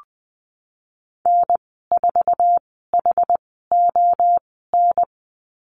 Morse code, heard as a keyed tone, about 20 wpm, spaced as for 15 wpm, 700 Hz. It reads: N4HON
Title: 076 Lesson 13g - Letter L - Call Signs 20wpm